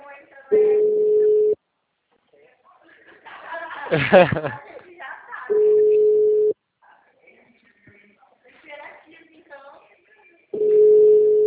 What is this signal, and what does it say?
Ringback tone (Europe, 425 Hz), with speech.